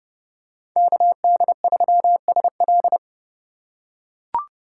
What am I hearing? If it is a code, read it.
Morse code, heard as a keyed tone, about 30 wpm, 700 Hz. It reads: KD3SL
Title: USA Call Signs - 3x Character Spacing - 30wpm